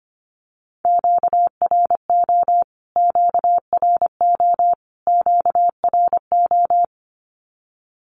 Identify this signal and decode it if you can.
Morse code, heard as a keyed tone, about 25 wpm, 700 Hz. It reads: QRO QRO QRO
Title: Q-Codes Repeated 3x 25wpm